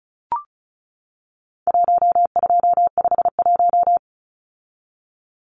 Morse code, heard as a keyed tone, about 34 wpm, 700 Hz. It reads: 1251
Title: Numbers - Contest Style - 4-Digit 35wpm